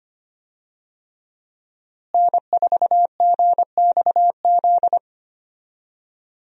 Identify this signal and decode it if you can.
Morse code, heard as a keyed tone, about 25 wpm, 700 Hz. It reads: N4GXZ
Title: USA Call Signs - 5 Characters - Repeated 3x - 25wpm